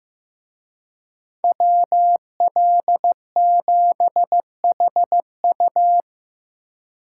Morse code, heard as a keyed tone, about 18 wpm, 700 Hz. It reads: WL7HU
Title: Speed Racing - USA Call Signs 15wpm